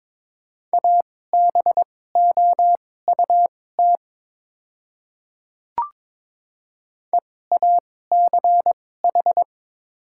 Morse code, heard as a keyed tone, about 22 wpm, spaced as for 16 wpm, 700 Hz. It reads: ABOUT EACH
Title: 2x Character Spacing - Top 100 Words - 22wpm